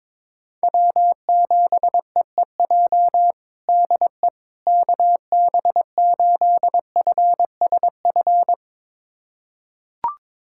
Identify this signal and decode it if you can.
Morse code, heard as a keyed tone, about 22 wpm, 700 Hz. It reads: W7EEJ DE KB8FSF
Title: USA Callsign DE Callsign 22wpm